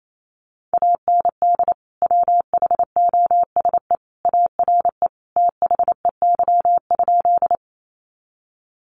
Morse code, heard as a keyed tone, about 28 wpm, 700 Hz. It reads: AND WHOSE ARE THEY?
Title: Sentences from Top 1200 Words 28wpm